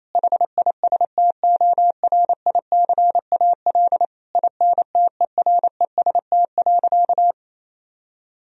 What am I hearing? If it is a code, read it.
Morse code, heard as a keyed tone, about 27 wpm, 700 Hz. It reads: HISTORICAL INTEREST.
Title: Sentences from Top 1000 Words - Review Part 4 - 28wpm